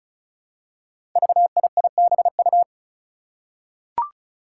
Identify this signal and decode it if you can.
Morse code, heard as a keyed tone, about 36 wpm, 700 Hz. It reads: VIIBU
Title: Five Challenging Letter-Number Combinations 35wpm